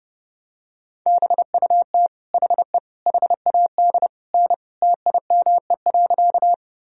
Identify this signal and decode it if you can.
Morse code, heard as a keyed tone, about 30 wpm, 700 Hz. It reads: BUT HE HAD N TIME.
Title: Sentences from Top 100 Words and Abbreviations 30wpm